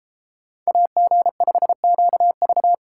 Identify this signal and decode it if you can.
Morse code, heard as a keyed tone, about 32 wpm, 700 Hz. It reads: AG5QV